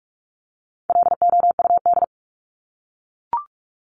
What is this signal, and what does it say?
Morse code, heard as a keyed tone, about 44 wpm, 700 Hz. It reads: LOUD
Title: Top 1751-2000 Words 45wpm